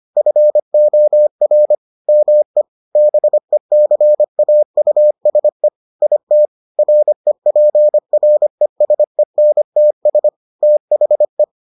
Morse code, heard as a keyed tone, about 25 wpm, 600 Hz. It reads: FOR ME BECAUSE IT REPRESENTS THE